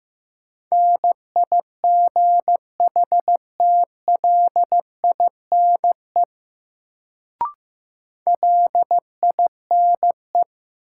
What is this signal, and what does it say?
Morse code, heard as a keyed tone, about 15 wpm, 700 Hz. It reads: NIGHTLINE LINE
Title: Sets of 5 Words - New Word Drill - 15wpm